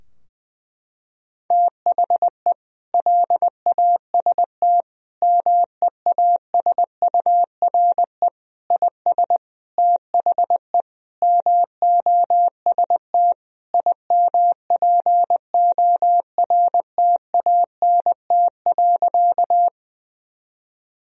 Morse code, heard as a keyed tone, about 20 wpm, 700 Hz. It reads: THE LAST MEASURE IS THE MOST IMPORTANT.
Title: Sentences from Top 1000 Words - Review Part 2 - 20wpm